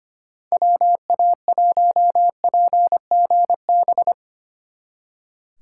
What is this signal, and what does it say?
Morse code, heard as a keyed tone, about 31 wpm, 700 Hz. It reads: WA1PGB
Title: Speed Racing - USA 2x3 Call Signs 25wpm